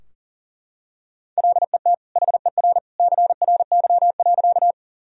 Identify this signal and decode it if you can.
Morse code, heard as a keyed tone, about 40 wpm, 700 Hz. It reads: LET HER CRY.
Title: Sentences from Top 1000 Words - Review Part 2 - 40wpm